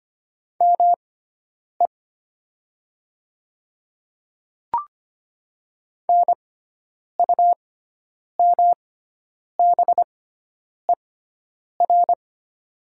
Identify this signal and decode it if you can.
Morse code, heard as a keyed tone, about 25 wpm, spaced as for 9 wpm, 700 Hz. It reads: ME NUMBER
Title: 6x Character Spacing - Top 200 Words - 25wpm